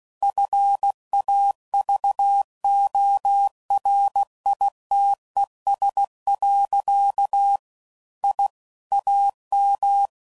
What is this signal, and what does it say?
Morse code, heard as a keyed tone, about 16 wpm, 800 Hz. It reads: FAVORITES. I AM
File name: CW Code Groups @15WPM/QSOs/WD0FBT Spaced